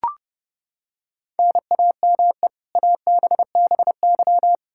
Morse code, heard as a keyed tone, about 30 wpm, 700 Hz. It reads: NAME ABBY
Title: Practice QSO - Exchange 1 - Beginning - 30wpm